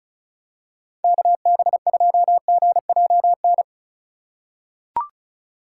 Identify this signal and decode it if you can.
Morse code, heard as a keyed tone, about 36 wpm, 700 Hz. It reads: KB2GJN
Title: USA Call Signs - 2x Character Spacing - 35wpm